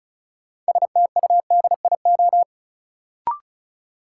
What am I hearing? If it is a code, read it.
Morse code, heard as a keyed tone, about 36 wpm, 700 Hz. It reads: STUDIO
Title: Sentences from Top 1200 Words - New Word Drill - 35wpm